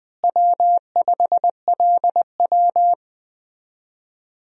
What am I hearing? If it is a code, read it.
Morse code, heard as a keyed tone, about 20 wpm, 700 Hz. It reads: W5LW
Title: USA Call Signs - 4 Characters - Repeated 3x - 20wpm